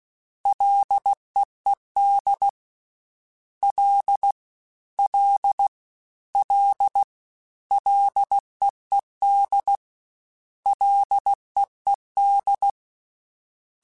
Morse code, heard as a keyed tone, about 16 wpm, 800 Hz. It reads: LEED L L L LEED LEED
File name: Ls / Single L m3s / Single L Prefixe 3x2 Twice @15